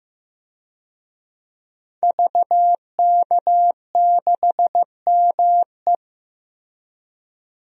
Morse code, heard as a keyed tone, about 15 wpm, 700 Hz. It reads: VK6ME
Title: DXCC-100 Call Signs - 5 Characters - Repeated 2x - 15wpm